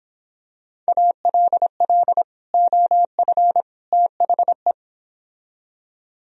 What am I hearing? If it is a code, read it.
Morse code, heard as a keyed tone, about 26 wpm, 700 Hz. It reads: ALL OF THE